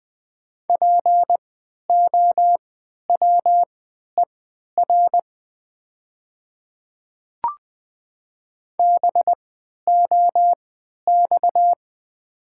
Morse code, heard as a keyed tone, about 20 wpm, spaced as for 11 wpm, 700 Hz. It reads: POWER BOX
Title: 3x Character Spacing - Top 500 Words - 20wpm